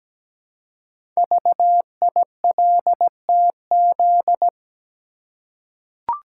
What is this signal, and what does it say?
Morse code, heard as a keyed tone, about 17 wpm, 700 Hz. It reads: VILTZ
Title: Code Groups - 5 Letters - 17wpm